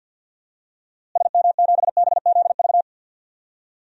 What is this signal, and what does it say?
Morse code, heard as a keyed tone, about 52 wpm, 700 Hz. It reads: SM7BZV
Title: DXCC-100 Call Signs - 6 Characters - Repeated 3x - 50wpm